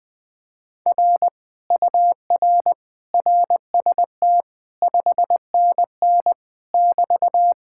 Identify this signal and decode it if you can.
Morse code, heard as a keyed tone, about 20 wpm, 700 Hz. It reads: R UR RST 5NN <BT>
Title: Speed-Racing Common QSO Phrases - 17wpm